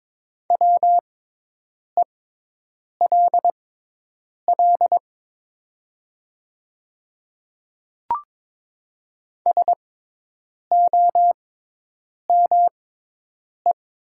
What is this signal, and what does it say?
Morse code, heard as a keyed tone, about 23 wpm, spaced as for 8 wpm, 700 Hz. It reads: WELL SOME
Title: 6x Character Spacing - Top 500 Words - 22wpm